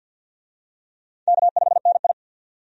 Morse code, heard as a keyed tone, about 47 wpm, 700 Hz. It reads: K5NI
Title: USA Call Signs - 4 Characters - Repeated 3x - 50wpm